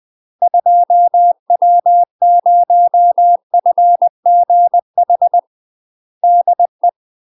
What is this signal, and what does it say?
Morse code, heard as a keyed tone, about 20 wpm, 700 Hz. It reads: 2W0FGH DE
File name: UK MASTER QSOs TRAINING / 20WPM / UK MASTER QSOs 0004